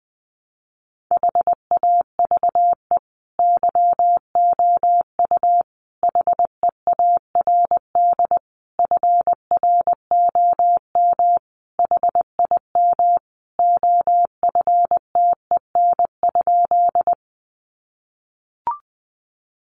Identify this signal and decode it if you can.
Morse code, heard as a keyed tone, about 20 wpm, 700 Hz. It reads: HAVE YOU HEARD FROM HIM OFTEN?
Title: Sentences from Top 400 Words 20wpm